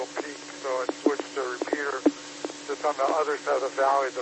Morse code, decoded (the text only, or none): none